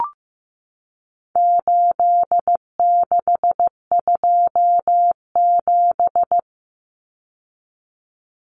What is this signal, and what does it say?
Morse code, heard as a keyed tone, about 15 wpm, 700 Hz. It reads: 8627
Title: Numbers - 4-Digit 15wpm